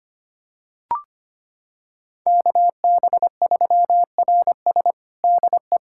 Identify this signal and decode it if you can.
Morse code, heard as a keyed tone, about 25 wpm, 700 Hz. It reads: KB3RS DE
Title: USA Callsign DE Callsign 25wpm